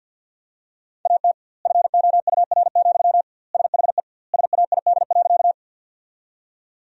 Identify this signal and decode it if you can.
Morse code, heard as a keyed tone, about 50 wpm, 700 Hz. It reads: AT FOUR, SHE SAID.
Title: 2x Word-Spacing - Sentences from Top 300 Words - 50wpm